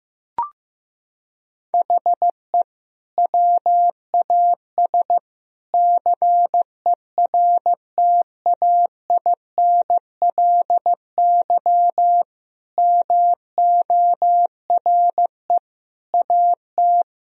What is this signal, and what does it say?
Morse code, heard as a keyed tone, about 15 wpm, 700 Hz. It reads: HE WAS CERTAINLY MORE AT
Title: Sentences from Top 1000 Words - Review Part 1 - 15wpm